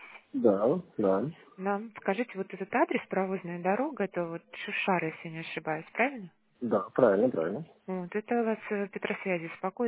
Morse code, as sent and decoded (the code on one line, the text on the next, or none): none